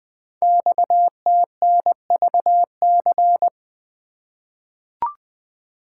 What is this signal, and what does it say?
Morse code, heard as a keyed tone, about 20 wpm, 700 Hz. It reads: XTNVC
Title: Code Groups - 5 Letter:Numbers - 20wpm